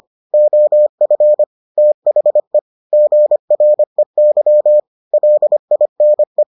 Morse code, heard as a keyed tone, about 25 wpm, 600 Hz. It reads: OF THE GREY LINE